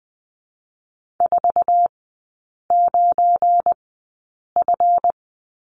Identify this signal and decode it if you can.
Morse code, heard as a keyed tone, about 20 wpm, spaced as for 8 wpm, 690 Hz. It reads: 49F
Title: 172 Lesson 28e - Number 9 - Three Characters 20wpm